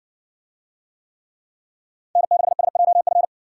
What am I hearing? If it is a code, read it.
Morse code, heard as a keyed tone, about 55 wpm, 700 Hz. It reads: N6SJF